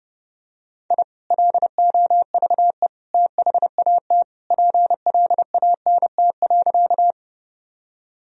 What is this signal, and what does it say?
Morse code, heard as a keyed tone, about 30 wpm, 700 Hz. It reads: I LOVE THAT PLANT.
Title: Sentences from Top 1000 Words - Review Part 1 - 30wpm